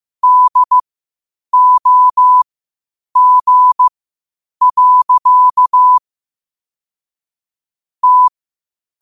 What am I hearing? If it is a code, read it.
Morse code, heard as a keyed tone, about 15 wpm, spaced as for 8 wpm, 1000 Hz. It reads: DOG. T
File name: qotdfiles / 8wpm